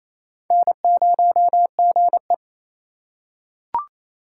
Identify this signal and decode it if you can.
Morse code, heard as a keyed tone, about 27 wpm, 700 Hz. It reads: N0GE
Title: USA Call Signs - 3x Character Spacing - 28wpm